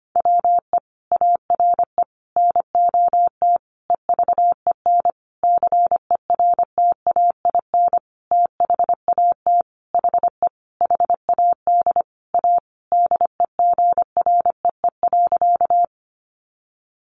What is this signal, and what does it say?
Morse code, heard as a keyed tone, about 25 wpm, 700 Hz. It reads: WE ARE NOT EVEN CERTAIN THAT HE HAD A DEGREE.
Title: Sentences from Top 1500 Words 25wpm